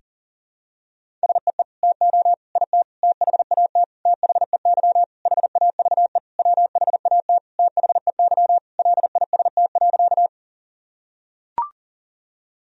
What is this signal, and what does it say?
Morse code, heard as a keyed tone, about 40 wpm, 700 Hz. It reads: SEE TO IT THAT THEY HAVE WHAT THEY LIST.
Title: Sentences from Top 1000 Words - Review Part 2 - 40wpm